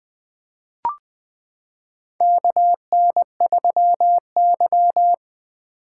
Morse code, heard as a keyed tone, about 20 wpm, 700 Hz. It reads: KN3Y
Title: 2023 SOTA Chaser Call Signs - 20wpm